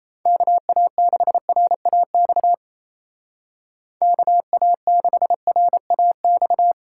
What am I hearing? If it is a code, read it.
Morse code, heard as a keyed tone, about 37 wpm, 700 Hz. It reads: KA6RAX KA6RAX